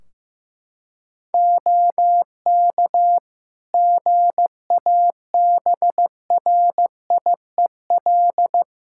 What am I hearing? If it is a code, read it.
Morse code, heard as a keyed tone, about 15 wpm, 700 Hz. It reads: OK GABRIEL
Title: Practice QSO - Exchange 4 - Ending - 15wpm